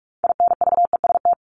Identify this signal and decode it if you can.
Morse code, heard as a keyed tone, about 44 wpm, 700 Hz. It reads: INVEST
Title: Top 1001-1250 Words 45wpm